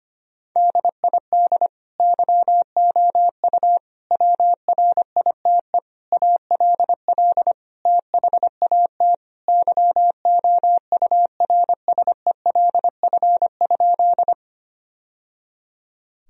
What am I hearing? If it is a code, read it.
Morse code, heard as a keyed tone, about 25 wpm, 700 Hz. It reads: DID YOU WRITE ALL THAT YOURSELF?
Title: Sentences from Top 1000 Words - Review Part 1 - 25wpm